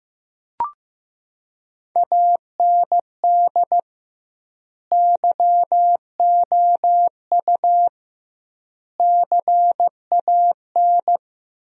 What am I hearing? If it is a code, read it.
Morse code, heard as a keyed tone, about 15 wpm, 700 Hz. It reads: AND YOU CAN
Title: 2x Word-Spacing - Sentences from Top 400 Words - 15wpm